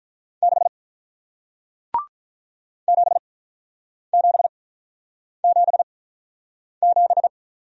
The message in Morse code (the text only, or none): 6 7777